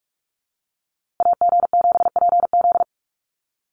Morse code, heard as a keyed tone, about 44 wpm, 690 Hz. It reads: AG7PZ
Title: USA Call Signs Repeated 2x 45wpm